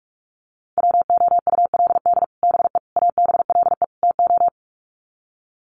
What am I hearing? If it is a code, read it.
Morse code, heard as a keyed tone, about 44 wpm, 700 Hz. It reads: WOULD BE ABLE TO
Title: Sets of 4 Words 45wpm